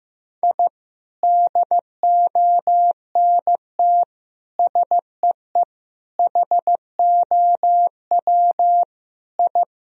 Morse code, heard as a keyed tone, about 16 wpm, 700 Hz. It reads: I DONT SEE HOW I
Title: Sentences from Top 1000 Words - Review Part 2 - 15wpm